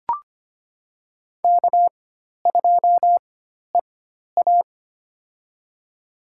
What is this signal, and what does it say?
Morse code, heard as a keyed tone, about 25 wpm, spaced as for 12 wpm, 700 Hz. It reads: K2EA
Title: USA Call Signs - 4x Character Spacing - 25wpm